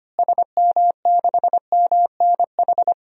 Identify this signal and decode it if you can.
Morse code, heard as a keyed tone, about 25 wpm, 700 Hz. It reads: SM6MNH
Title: DXCC-100 Call Signs - 6 Characters - Repeated 2x - 25wpm